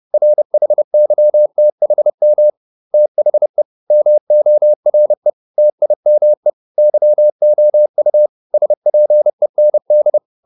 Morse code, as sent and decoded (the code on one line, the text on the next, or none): .-. .... -.-- - .... -- / - .... / -- --- .-. / - .. -- . / -.-- --- ..- / ... .--. -. -..
RHYTHM THE MORE TIME YOU SPEND